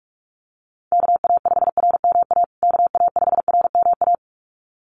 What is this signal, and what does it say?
Morse code, heard as a keyed tone, about 44 wpm, 700 Hz. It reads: KA5RMA KA5RMA
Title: USA Call Signs Repeated 2x 45wpm